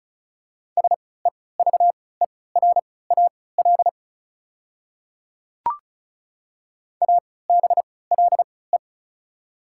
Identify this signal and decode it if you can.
Morse code, heard as a keyed tone, about 34 wpm, spaced as for 20 wpm, 690 Hz. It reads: SEVERAL ABLE
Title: 3x Character Spacing - Top 500 Words - 35wpm